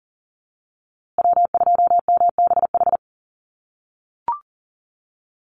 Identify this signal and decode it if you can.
Morse code, heard as a keyed tone, about 40 wpm, 700 Hz. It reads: W2MBH
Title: USA Call Signs Repeated 2x 40wpm